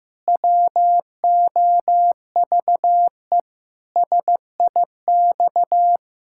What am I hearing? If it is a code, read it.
Morse code, heard as a keyed tone, about 15 wpm, 700 Hz. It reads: WOVE SIX
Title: Pangram Challenge v2 - 15wpm